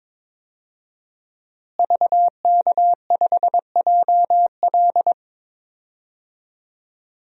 Morse code, heard as a keyed tone, about 22 wpm, 700 Hz. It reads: VK5JL
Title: DXCC-100 Call Signs - 5 Characters - Repeated 3x - 22wpm